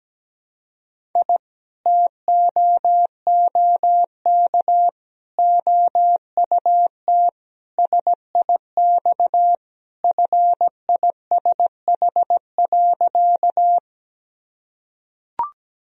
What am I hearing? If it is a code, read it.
Morse code, heard as a keyed tone, about 17 wpm, 700 Hz. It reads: I TOOK OUT SIX FISH.
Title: Sentences from Top 1000 Words - Review Part 2 - 17wpm